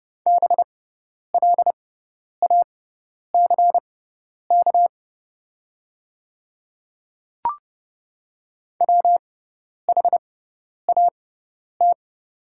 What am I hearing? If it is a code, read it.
Morse code, heard as a keyed tone, about 29 wpm, spaced as for 10 wpm, 690 Hz. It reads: BLACK WHAT
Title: 6x Character Spacing - Top 500 Words - 30wpm